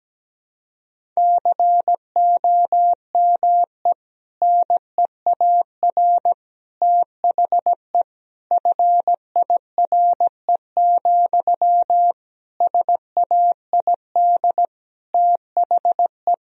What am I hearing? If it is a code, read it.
Morse code, heard as a keyed tone, about 17 wpm, 700 Hz. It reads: COME NEAR THE FIRE, SAID THE
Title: Sentences from Top 1000 Words - Review Part 1 - 17wpm